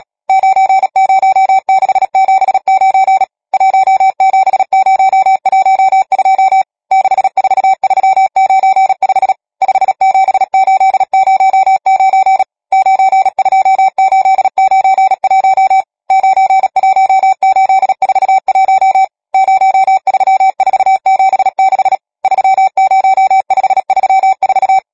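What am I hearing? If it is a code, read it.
Morse code, heard as a keyed tone, about 37 wpm, 770 Hz. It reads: 90679 17012 64395 57809 92891 91841 03476 30534
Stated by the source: Morse code number practice, machine-sent training text (Nácvik telegrafie, OK1FCB).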